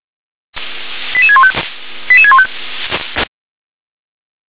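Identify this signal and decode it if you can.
Five-tone selective call (ZVEI), heard as quick runs of five tones, tones of about 80 ms. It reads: 88615 88615
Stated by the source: ZVEI five-tone selective call received on an SDR.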